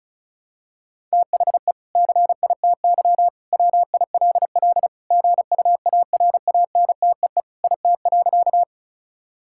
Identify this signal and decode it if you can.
Morse code, heard as a keyed tone, about 36 wpm, 700 Hz. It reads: THE CITY WILL GUARANTEE IT.
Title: Sentences from Top 1600 Words 35wpm